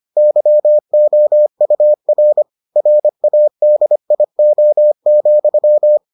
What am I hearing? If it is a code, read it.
Morse code, heard as a keyed tone, about 25 wpm, 600 Hz. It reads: YOUR RADIO,